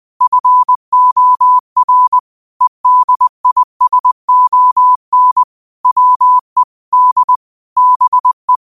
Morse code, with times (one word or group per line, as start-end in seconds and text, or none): 0.2-2.2 FOR
2.6-5.4 EDISON
5.9-6.6 WE
6.9-7.4 D
7.8-8.6 BE